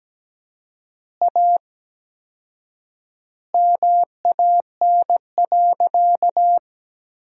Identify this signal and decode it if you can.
Morse code, heard as a keyed tone, about 17 wpm, 700 Hz. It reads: A MAN.
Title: 4x Word-Spacing - Sentences from Top 300 Words - 17wpm